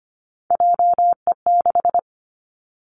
Morse code, heard as a keyed tone, about 25 wpm, 700 Hz. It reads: JE6
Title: Three Challenging Letter-Number Combinations 25wpm